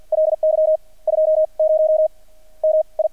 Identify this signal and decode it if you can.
Morse code, heard as a keyed tone, about 50 wpm, spaced as for 37 wpm, 630 Hz. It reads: LY 20 MI
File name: archived / CRSnet